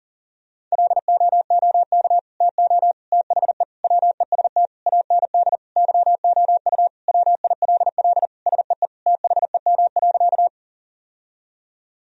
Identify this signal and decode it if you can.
Morse code, heard as a keyed tone, about 40 wpm, 700 Hz. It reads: LOOK TO THE WEST AND YOU WILL SEE THEM.
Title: Sentences from Top 1000 Words - Review Part 2 - 40wpm